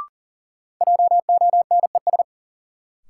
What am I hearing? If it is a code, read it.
Morse code, heard as a keyed tone, about 37 wpm, 700 Hz. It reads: JONES
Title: Sets of 5 Words - New Word Drill - 40wpm